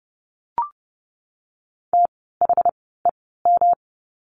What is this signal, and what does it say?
Morse code, heard as a keyed tone, about 30 wpm, spaced as for 17 wpm, 700 Hz. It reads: THEM
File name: course / 30:10wpm / 230 Lesson 37f - Letter x - Words 30wpm